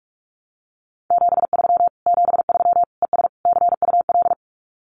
Morse code, heard as a keyed tone, about 44 wpm, 700 Hz. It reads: 73 73 ES CUL
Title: Common QSO Phrases Encore 45wpm